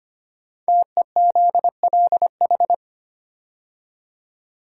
Morse code, heard as a keyed tone, about 25 wpm, 700 Hz. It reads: TEZLH
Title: Code Groups - 5 Letters - 25wpm